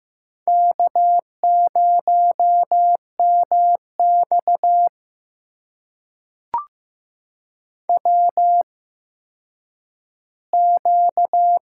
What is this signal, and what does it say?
Morse code, heard as a keyed tone, about 15 wpm, spaced as for 9 wpm, 700 Hz. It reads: K0MX WQ
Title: USA Call Signs - 4 Characters - 8x Character Spacing - 15wpm